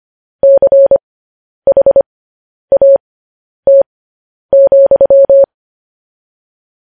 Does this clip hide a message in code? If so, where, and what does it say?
Yes: Morse code from 0.4 to 5.4 s, heard as a keyed tone, about 25 wpm, spaced as for 10 wpm, 550 Hz. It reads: CHAT,